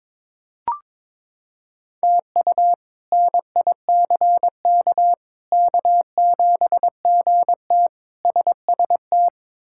Morse code, heard as a keyed tone, about 21 wpm, 700 Hz. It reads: TU NICK K7GT SST
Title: K1USN SST QSO Practice - 22wpm